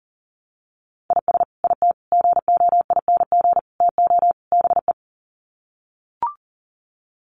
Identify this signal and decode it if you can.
Morse code, heard as a keyed tone, about 40 wpm, 700 Hz. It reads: IS IT GOING TO BE
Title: Sets of 5 Words Encore2 40wpm